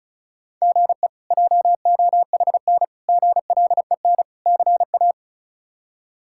Morse code, heard as a keyed tone, about 36 wpm, 700 Hz. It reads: GE JOHN GLEN CA
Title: K1USN SST QSO Practice - 35wpm